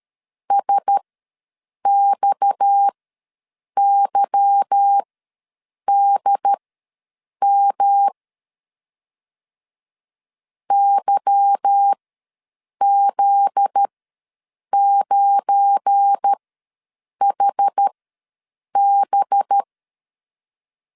Morse code, heard as a keyed tone, about 13 wpm, spaced as for 7 wpm, 800 Hz. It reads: SXYDM YZ9HB
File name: LEZIONI / 8 YJ45 / 8 Y ws 12wpm 3409